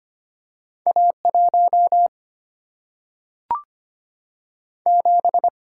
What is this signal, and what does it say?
Morse code, heard as a keyed tone, about 25 wpm, 700 Hz. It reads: A1 7
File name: Two Letter-Number Combinations - 25wpm